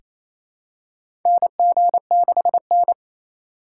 Morse code, heard as a keyed tone, about 28 wpm, 700 Hz. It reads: NG6N
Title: USA Call Signs - 4 Characters - 4x Character Spacing - 28wpm